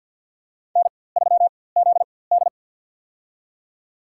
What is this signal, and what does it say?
Morse code, heard as a keyed tone, about 51 wpm, spaced as for 23 wpm, 700 Hz. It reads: N3ZD
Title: USA Call Signs - 4x Character Spacing - 50wpm